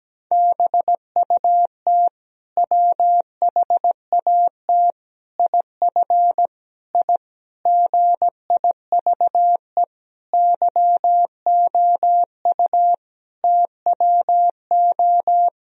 Morse code, heard as a keyed tone, about 17 wpm, 700 Hz. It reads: BUT WHAT IF I GIVE YOU TWO
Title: Sentences from Top 1000 Words - Review Part 2 - 17wpm